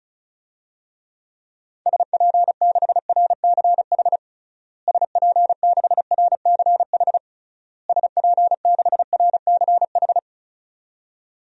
Morse code, heard as a keyed tone, about 36 wpm, 700 Hz. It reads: SP6RCH SP6RCH SP6RCH
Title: DXCC-100 Call Signs - 6 Characters - Repeated 3x - 35wpm